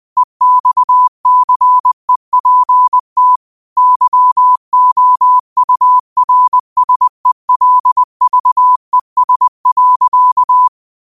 Morse code, heard as a keyed tone, about 20 wpm, 1000 Hz. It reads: EXCEPT YOURSELVES.
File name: qotdfiles / 20wpm